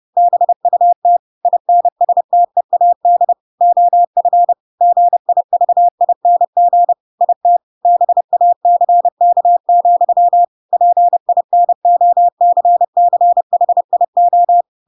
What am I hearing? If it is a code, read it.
Morse code, heard as a keyed tone, about 30 wpm, 700 Hz. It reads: BUT INSTEAD OF GIVING IT BACK, PINOCCHIO
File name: Pinocchio / Pinocchio ch3 30 30 ik1hge